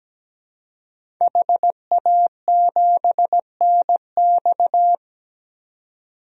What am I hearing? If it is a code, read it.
Morse code, heard as a keyed tone, about 17 wpm, 700 Hz. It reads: HA7NX
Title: DXCC-100 Call Signs - 5 Characters - 17wpm